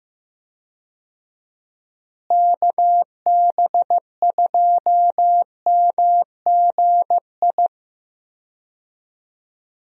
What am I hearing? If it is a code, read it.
Morse code, heard as a keyed tone, about 15 wpm, 700 Hz. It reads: KB2MGI
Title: USA Call Signs - 6 Characters - Repeated 2x - 15wpm